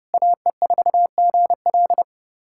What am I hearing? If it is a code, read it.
Morse code, heard as a keyed tone, about 30 wpm, 700 Hz. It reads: AE4GL
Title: USA Callsign DE Callsign - 5 Characters - 30wpm